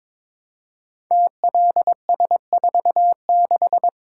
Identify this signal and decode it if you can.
Morse code, heard as a keyed tone, about 23 wpm, 700 Hz. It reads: TLS46
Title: Code Groups - 5 Letter:Numbers - 22wpm